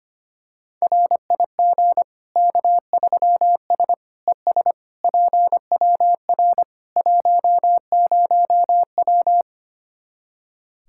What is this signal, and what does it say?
Morse code, heard as a keyed tone, about 25 wpm, 700 Hz. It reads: RIG K3S ES PWR 10W
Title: Practice QSO - Exchange 2 - Info - 25wpm